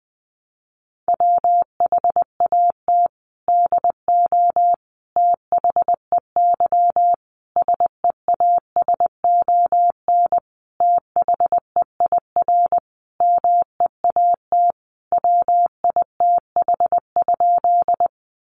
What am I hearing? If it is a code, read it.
Morse code, heard as a keyed tone, about 20 wpm, 700 Hz. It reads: WHAT DO THEY SEASON THEIR MEAT WITH?
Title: Sentences from Top 1400 Words 20wpm